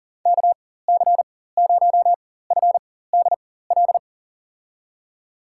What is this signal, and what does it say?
Morse code, heard as a keyed tone, about 37 wpm, spaced as for 18 wpm, 700 Hz. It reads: KC0FDL
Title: USA Call Signs - 4x Character Spacing - 40wpm